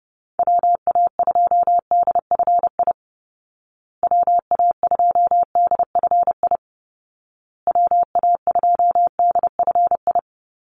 Morse code, heard as a keyed tone, about 30 wpm, 700 Hz. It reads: WA2DFI WA2DFI WA2DFI
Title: Call Signs Repeated 2x - 30wpm